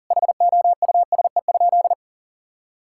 Morse code, heard as a keyed tone, about 40 wpm, 700 Hz. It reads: HOUSE?
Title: Sentences from Top 1000 Words - Review Part 6 - 40wpm